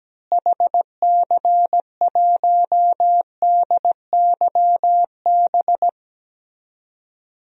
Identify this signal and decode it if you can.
Morse code, heard as a keyed tone, about 17 wpm, 700 Hz. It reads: HC1DYB
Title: DXCC-100 Call Signs - 6 Characters - Repeated 2x - 17wpm